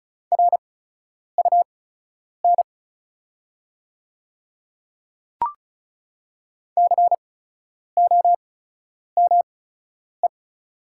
Morse code, heard as a keyed tone, about 36 wpm, spaced as for 10 wpm, 700 Hz. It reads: RUN COME
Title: 8x Character Spacing - Top 300 Words - 35wpm